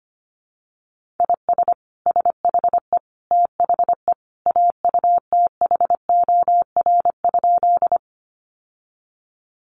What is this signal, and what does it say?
Morse code, heard as a keyed tone, about 25 wpm, 700 Hz. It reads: IS SHE THE AUTHOR?
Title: Sentences from Top 1200 Words 25wpm